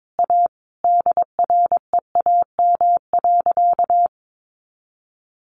Morse code, heard as a keyed tone, about 22 wpm, 700 Hz. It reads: A DREAM.
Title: Sentences from Top 1500 Words 22wpm